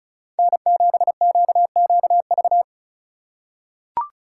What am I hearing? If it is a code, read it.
Morse code, heard as a keyed tone, about 36 wpm, 700 Hz. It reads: N7QQV
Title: USA Call Signs - 5 Characters - 4x Character Spacing - 35wpm